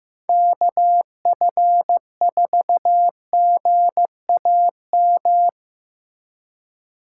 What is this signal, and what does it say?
Morse code, heard as a keyed tone, about 15 wpm, 700 Hz. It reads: KF4GAM
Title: USA Call Signs - 6 Characters - 4x Character Spacing - 15wpm